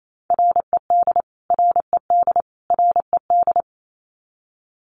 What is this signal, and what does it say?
Morse code, heard as a keyed tone, about 28 wpm, 700 Hz. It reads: RED RED RED
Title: Top 500 Words Repeated 28wpm